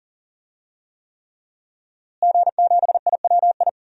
Morse code, heard as a keyed tone, about 40 wpm, 700 Hz. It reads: G7IWI
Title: DXCC-100 Call Signs - 5 Characters - Repeated 3x - 40wpm